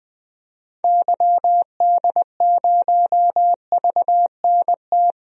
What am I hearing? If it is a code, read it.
Morse code, heard as a keyed tone, about 20 wpm, 700 Hz. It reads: YD0VNT
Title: DXCC-100 Call Signs - 6 Characters - 20wpm